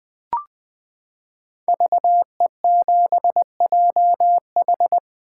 Morse code, heard as a keyed tone, about 20 wpm, 700 Hz. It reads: VE7JH
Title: 2023 SOTA Chaser Call Signs - 20wpm